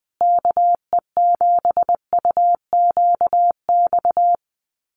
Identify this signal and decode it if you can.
Morse code, heard as a keyed tone, about 20 wpm, 700 Hz. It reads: KE7UQX
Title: USA Call Signs Repeated 2x 20wpm